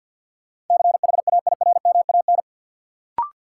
Morse code, heard as a keyed tone, about 52 wpm, 700 Hz. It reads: CHAIRMAN
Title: Sets of 3 Words - New Word Drill - 50wpm